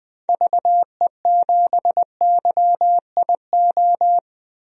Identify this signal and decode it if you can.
Morse code, heard as a keyed tone, about 20 wpm, 700 Hz. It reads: VE7YIO
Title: DXCC-100 Call Signs - 6 Characters - 20wpm